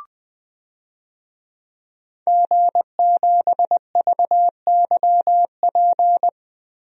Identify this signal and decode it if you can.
Morse code, heard as a keyed tone, about 21 wpm, 700 Hz. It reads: G7VYP
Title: DXCC-100 Call Signs - 5 Characters - Repeated 2x - 20wpm